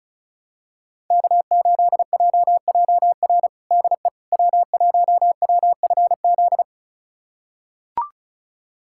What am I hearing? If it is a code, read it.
Morse code, heard as a keyed tone, about 34 wpm, 700 Hz. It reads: K8JJR DE W1WFZ
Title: USA Callsign DE Callsign - 5 Characters - 35wpm